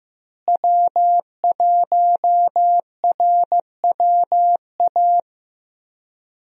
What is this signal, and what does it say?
Morse code, heard as a keyed tone, about 15 wpm, 700 Hz. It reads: W1RWA
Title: USA Call Signs - 5 Characters - Repeated 2x - 15wpm